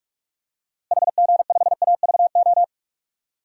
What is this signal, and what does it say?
Morse code, heard as a keyed tone, about 42 wpm, 700 Hz. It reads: HG5AVO
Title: DXCC-100 Call Signs - 6 Characters - Repeated 2x - 45wpm